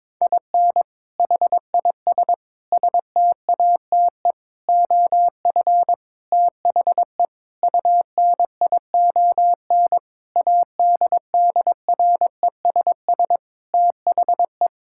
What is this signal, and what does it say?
Morse code, heard as a keyed tone, about 22 wpm, 700 Hz. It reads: IN HIS STATE OF THE UNION ADDRESS THE
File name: Sentences with 7-Word Phrases - 22wpm